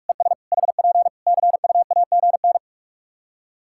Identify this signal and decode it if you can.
Morse code, heard as a keyed tone, about 43 wpm, 700 Hz. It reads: ES HP CUAGN
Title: K1USN SST QSO Practice 45wpm